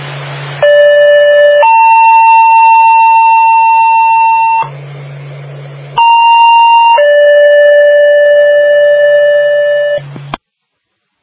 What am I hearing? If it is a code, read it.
Two-tone page, heard as a one-second tone, then a three-second tone, about 580 then 900 Hz, about 930 then 580 Hz. It is heard from 0.6 to 10.0 s.